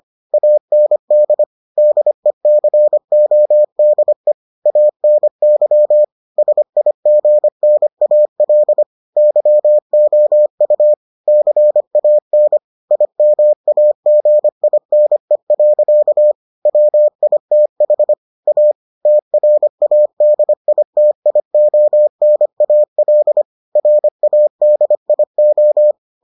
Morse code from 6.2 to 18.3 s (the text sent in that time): SIGNAL YOU CAN IMAGINE. WITH